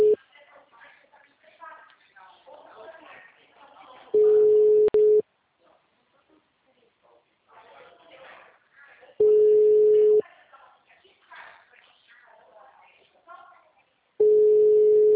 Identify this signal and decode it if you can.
Ringback tone (Europe, 425 Hz), with speech.